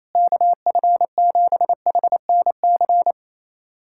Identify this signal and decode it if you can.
Morse code, heard as a keyed tone, about 28 wpm, 700 Hz. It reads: KF7HNC
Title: USA Callsign DE Callsign - 6 Characters - 28wpm